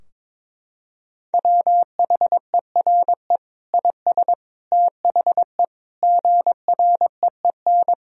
Morse code, heard as a keyed tone, about 22 wpm, 700 Hz. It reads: WHERE IS THE GREEN
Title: Sentences from Top 1000 Words - Review Part 2 - 22wpm